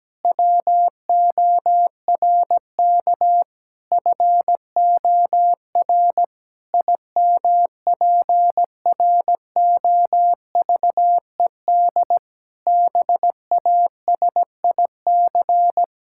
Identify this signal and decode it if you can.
Morse code, heard as a keyed tone, about 17 wpm, 700 Hz. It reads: WORK FOR IMPROVED BASIC
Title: Pangram Challenge v2 - 17wpm